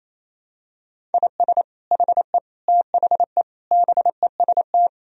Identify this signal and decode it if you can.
Morse code, heard as a keyed tone, about 28 wpm, 700 Hz. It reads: IS HE THE BEST
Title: Sentences from Top 1000 Words - Review Part 4 - 28wpm